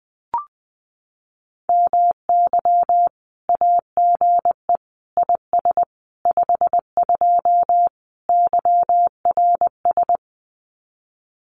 Morse code, heard as a keyed tone, about 20 wpm, 700 Hz. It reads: MY AGE IS 52 YRS
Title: Common QSO Phrases Encore 20wpm